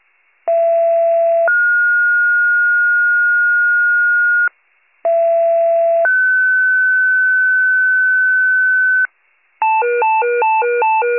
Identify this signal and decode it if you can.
Two-tone page, heard as a one-second tone, then a three-second tone, about 670 then 1430 Hz, about 670 then 1500 Hz. It is heard from 0.5 to 9.1 s.